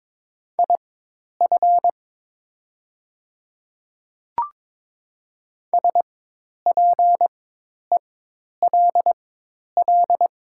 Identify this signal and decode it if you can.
Morse code, heard as a keyed tone, about 22 wpm, spaced as for 10 wpm, 700 Hz. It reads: IF SPELL
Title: 4x Character Spacing - Top 200 Words - 22wpm